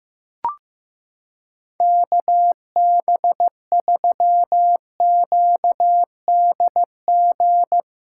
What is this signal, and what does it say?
Morse code, heard as a keyed tone, about 15 wpm, 700 Hz. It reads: KB3QDG
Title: USA Callsign DE Callsign - 6 Characters - 15wpm